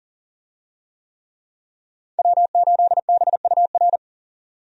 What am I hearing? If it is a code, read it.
Morse code, heard as a keyed tone, about 40 wpm, 700 Hz. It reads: W8BUR
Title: USA Call Signs - 5 Characters - 40wpm